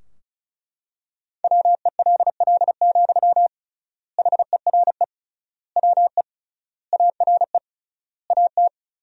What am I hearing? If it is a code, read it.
Morse code, heard as a keyed tone, about 36 wpm, 700 Hz. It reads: WELL, HERE WE ARE AT
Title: 3x Word-Spacing - Sentences from Top 300 Words - 35wpm